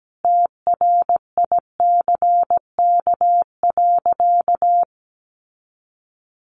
Morse code, heard as a keyed tone, about 17 wpm, 700 Hz. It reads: TRICK.
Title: Sentences from Top 1500 Words 17wpm